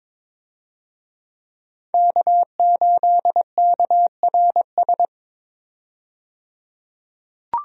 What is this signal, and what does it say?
Morse code, heard as a keyed tone, about 22 wpm, 700 Hz. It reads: K8KRS
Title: USA Call Signs - 5 Characters - 22wpm